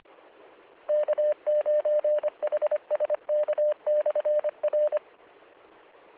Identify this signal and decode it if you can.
Morse code, heard as a keyed tone, about 25 wpm, 600 Hz. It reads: K9HSK/R